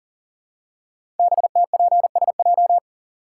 Morse code, heard as a keyed tone, about 42 wpm, 700 Hz. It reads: BTPSJ
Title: Code Groups - 5 Letters - 40wpm